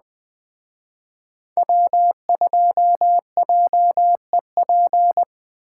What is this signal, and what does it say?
Morse code, heard as a keyed tone, about 21 wpm, 700 Hz. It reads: W2JEP